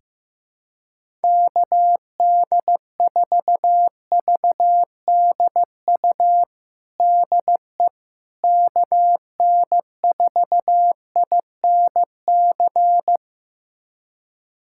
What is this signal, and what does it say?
Morse code, heard as a keyed tone, about 15 wpm, 700 Hz. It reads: KD4VDU DE KN4INC
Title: USA Callsign DE Callsign - 6 Characters - 15wpm